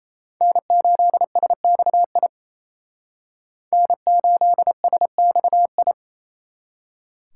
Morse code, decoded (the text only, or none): N8SXI N8SXI